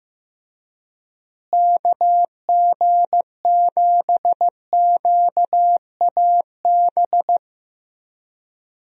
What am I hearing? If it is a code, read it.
Morse code, heard as a keyed tone, about 15 wpm, 700 Hz. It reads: KG7QAB
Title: USA Call Signs - 6 Characters - Repeated 3x - 15wpm